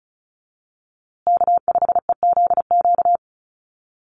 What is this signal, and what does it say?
Morse code, heard as a keyed tone, about 34 wpm, 700 Hz. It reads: K5EZQ
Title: USA Call Signs Repeated 2x 35wpm